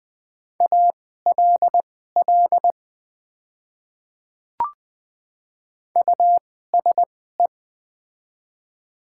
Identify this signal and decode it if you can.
Morse code, heard as a keyed tone, about 20 wpm, spaced as for 15 wpm, 700 Hz. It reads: ALL USE
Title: 2x Character Spacing - Top 100 Words - 20wpm